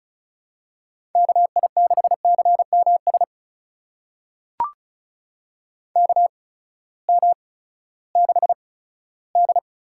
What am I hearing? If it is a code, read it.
Morse code, heard as a keyed tone, about 36 wpm, spaced as for 18 wpm, 700 Hz. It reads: KI6CMS KM6D
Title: USA Call Signs - 6 Characters - 8x Character Spacing - 35wpm